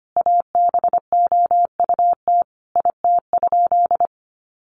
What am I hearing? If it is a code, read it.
Morse code, heard as a keyed tone, about 25 wpm, 700 Hz. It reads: ABOUT IT?
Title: Sentences from Top 1500 Words 25wpm